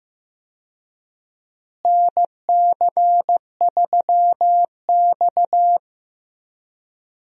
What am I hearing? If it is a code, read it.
Morse code, heard as a keyed tone, about 15 wpm, 700 Hz. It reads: NC3X
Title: USA Call Signs - 4 Characters - Repeated 2x - 15wpm